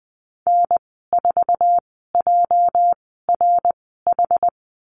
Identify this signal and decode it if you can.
Morse code, heard as a keyed tone, about 20 wpm, spaced as for 14 wpm, 700 Hz. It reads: N4JRH
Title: 207 Lesson 33g - Letter j - Call Signs 20wpm